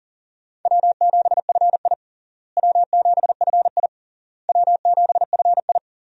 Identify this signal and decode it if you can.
Morse code, heard as a keyed tone, about 40 wpm, 700 Hz. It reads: W7FI W7FI W7FI
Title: USA Call Signs - 4 Characters - Repeated 3x - 40wpm